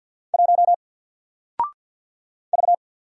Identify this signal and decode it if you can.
Morse code, heard as a keyed tone, about 50 wpm, 700 Hz. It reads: 1 V
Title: Single Letter-Number 50wpm